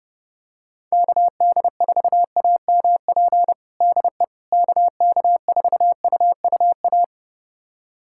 Morse code, heard as a keyed tone, about 32 wpm, 700 Hz. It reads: KD4AMP DE KK4UUA
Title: USA Callsign DE Callsign 30wpm